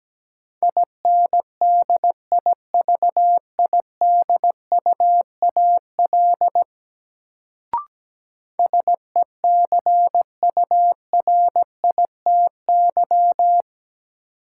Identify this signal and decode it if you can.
Morse code, heard as a keyed tone, about 17 wpm, 700 Hz. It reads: INDIVIDUAL SECURITY
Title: Sentences from Top 1000 Words - Review Part 4 - New Word Drill - 17wpm